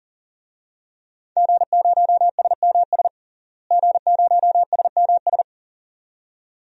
Morse code, heard as a keyed tone, about 40 wpm, 700 Hz. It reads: G0SMS G0SMS
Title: DXCC-100 Call Signs - 5 Characters - Repeated 2x - 40wpm